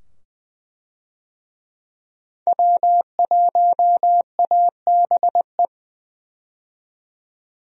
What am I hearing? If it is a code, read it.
Morse code, heard as a keyed tone, about 20 wpm, 700 Hz. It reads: W1ABE
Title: USA Call Signs - 5 Characters - 20wpm